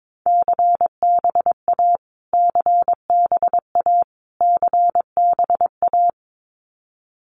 Morse code, heard as a keyed tone, about 22 wpm, 700 Hz. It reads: CBA CBA CBA
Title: Abbreviations Repeated 3x - 22wpm